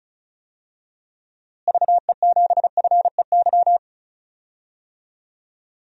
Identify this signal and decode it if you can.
Morse code, heard as a keyed tone, about 34 wpm, 700 Hz. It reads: VE7FEY
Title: DXCC-100 Call Signs - 6 Characters - Repeated 3x - 35wpm